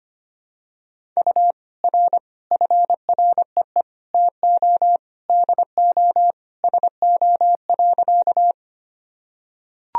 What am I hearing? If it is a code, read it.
Morse code, heard as a keyed tone, about 25 wpm, 700 Hz. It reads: U R FREE TO DO SO.
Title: Sentences from Top 500 Words and Abbreviations 25wpm